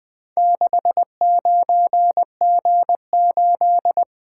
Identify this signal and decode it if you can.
Morse code, heard as a keyed tone, about 20 wpm, 700 Hz. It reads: 69G8
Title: Four Letter-Number Combinations - 20wpm